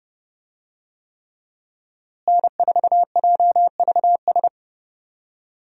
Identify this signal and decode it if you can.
Morse code, heard as a keyed tone, about 32 wpm, 700 Hz. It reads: N4JVS
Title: USA Call Signs - 5 Characters - Repeated 3x - 30wpm